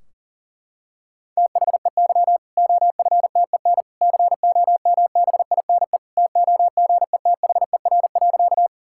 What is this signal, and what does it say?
Morse code, heard as a keyed tone, about 40 wpm, 700 Hz. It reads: THEY OFTEN COMBINE TOGETHER.
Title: Sentences from Top 1600 Words 40wpm